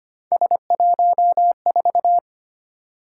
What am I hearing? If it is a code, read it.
Morse code, heard as a keyed tone, about 25 wpm, 700 Hz. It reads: S14
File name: Three Letter-Number Combinations - 25wpm